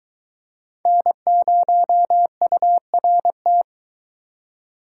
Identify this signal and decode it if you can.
Morse code, heard as a keyed tone, about 22 wpm, 700 Hz. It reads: N0URT